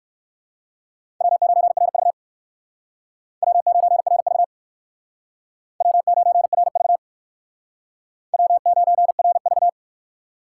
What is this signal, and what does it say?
Morse code, heard as a keyed tone, about 59 wpm, 700 Hz. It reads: W9RV W9RV W9RV W9RV